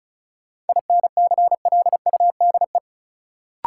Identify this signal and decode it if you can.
Morse code, heard as a keyed tone, about 36 wpm, 700 Hz. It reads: INCLUDE
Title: Sentences from Top 1000 Words - Review Part 3 - New Word Drill - 35wpm